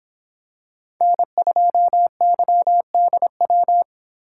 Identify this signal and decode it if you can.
Morse code, heard as a keyed tone, about 24 wpm, 700 Hz. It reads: N2YDW